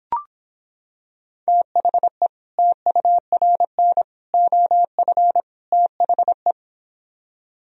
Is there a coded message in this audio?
Yes: Morse code from 1.5 to 6.5 s, heard as a keyed tone, about 26 wpm, 700 Hz. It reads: THE TURN OF THE